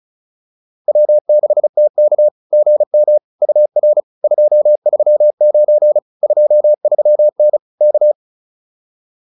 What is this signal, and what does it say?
Morse code, heard as a keyed tone, about 36 wpm, 600 Hz. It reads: W6TK GM UR 239 23N K
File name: POTA106 35